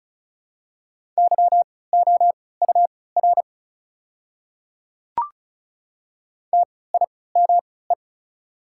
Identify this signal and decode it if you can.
Morse code, heard as a keyed tone, about 34 wpm, spaced as for 20 wpm, 700 Hz. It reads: YOUR TIME
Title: 3x Character Spacing - Top 300 Words - 35wpm